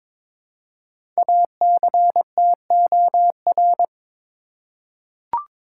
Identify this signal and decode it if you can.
Morse code, heard as a keyed tone, about 22 wpm, 700 Hz. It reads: ACTOR
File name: 5 Letter Words - First 3 Letters Given - 22wpm